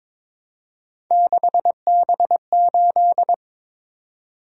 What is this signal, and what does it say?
Morse code, heard as a keyed tone, about 22 wpm, 700 Hz. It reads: 6B8
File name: practice / 22wpm / Three Letter-Number Combinations - 22wpm